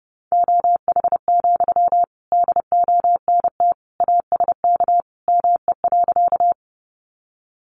Morse code, heard as a keyed tone, about 30 wpm, 700 Hz. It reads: OH, DONT ASK ME.
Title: Sentences from Top 500 Words 30wpm